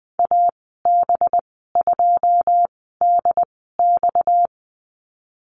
Morse code, heard as a keyed tone, about 20 wpm, spaced as for 15 wpm, 700 Hz. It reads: AB2DX
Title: 231 Lesson 37g - Letter x - Call Signs 20wpm